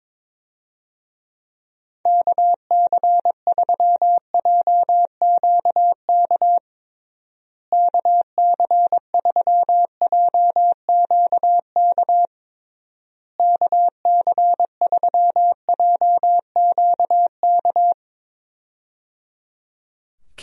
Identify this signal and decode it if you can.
Morse code, heard as a keyed tone, about 22 wpm, 700 Hz. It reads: KC3JQK KC3JQK KC3JQK
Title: USA Call Signs - 6 Characters - Repeated 3x - 22wpm